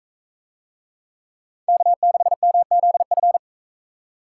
Morse code, heard as a keyed tone, about 38 wpm, 700 Hz. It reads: K6MZF